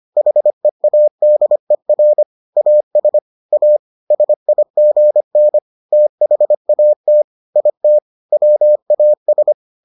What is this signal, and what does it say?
Morse code, heard as a keyed tone, about 25 wpm, 600 Hz. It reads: HEADER AS A SIGN THAT IT WAS